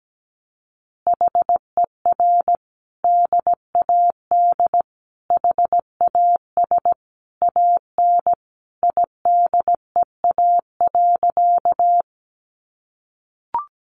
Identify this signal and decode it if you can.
Morse code, heard as a keyed tone, about 17 wpm, 700 Hz. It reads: HER DAD HAS AN IDEA.
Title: Easy Phrases 17wpm